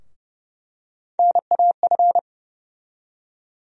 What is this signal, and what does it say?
Morse code, heard as a keyed tone, about 30 wpm, 700 Hz. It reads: NAF
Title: USA Call Sign Postfixes - 3 Letters 30wpm